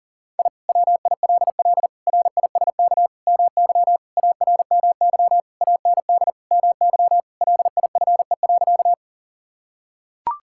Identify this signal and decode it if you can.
Morse code, heard as a keyed tone, about 40 wpm, 700 Hz. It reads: I WILL RISK MY ARMY AND MY LIFE.
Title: Sentences from Top 1000 Words - Review Part 5 - 40wpm